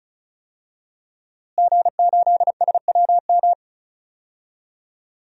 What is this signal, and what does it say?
Morse code, heard as a keyed tone, about 34 wpm, 700 Hz. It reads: G8SWM
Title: DXCC-100 Call Signs - 5 Characters - Repeated 3x - 35wpm